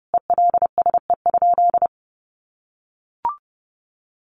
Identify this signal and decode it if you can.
Morse code, heard as a keyed tone, about 30 wpm, 700 Hz. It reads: ELSE?
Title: Sentences from Top 900 Words 30wpm